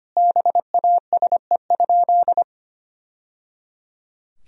Morse code, heard as a keyed tone, about 25 wpm, 700 Hz. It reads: BASE?
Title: Sentences from Top 1000 Words - Review Part 1 - 25wpm